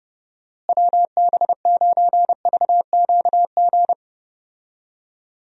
Morse code, heard as a keyed tone, about 30 wpm, 700 Hz. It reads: WB9VQG
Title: USA Call Signs - 6 Characters - Repeated 2x - 30wpm